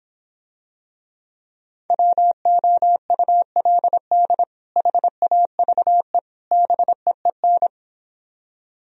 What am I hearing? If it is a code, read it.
Morse code, heard as a keyed tone, about 26 wpm, 700 Hz. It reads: WOULD HAVE BEEN